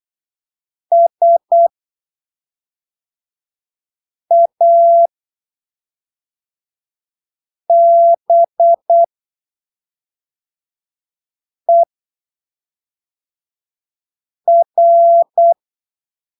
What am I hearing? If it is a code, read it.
Morse code, heard as a keyed tone, about 8 wpm, spaced as for 3 wpm, 680 Hz. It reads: SABER
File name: arquivos / texto / texto 15 08ppm